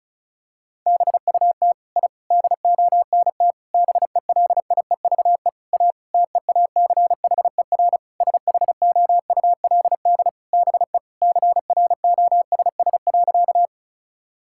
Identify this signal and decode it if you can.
Morse code, heard as a keyed tone, about 36 wpm, 700 Hz. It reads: BUT I DONT BELIEVE A TEACHER SHOULD BE CROSS.
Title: Sentences from Top 1000 Words - Review Part 6 - 35wpm